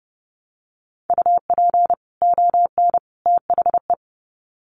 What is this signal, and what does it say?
Morse code, heard as a keyed tone, about 30 wpm, 700 Hz. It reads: UP ON THE
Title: Sets of 3 Words 30wpm